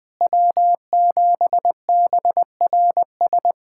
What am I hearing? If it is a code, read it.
Morse code, heard as a keyed tone, about 20 wpm, 700 Hz. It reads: W7BRS
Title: USA Callsign DE Callsign - 5 Characters - 20wpm